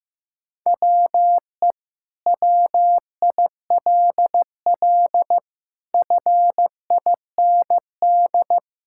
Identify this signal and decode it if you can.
Morse code, heard as a keyed tone, about 15 wpm, 700 Hz. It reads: WE WILL FIND
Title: Sentences from Top 1000 Words - Review Part 5 - 15wpm